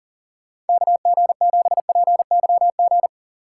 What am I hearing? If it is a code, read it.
Morse code, heard as a keyed tone, about 42 wpm, 700 Hz. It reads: KG7PYG
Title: USA Call Signs - 6 Characters - 6x Character Spacing - 40wpm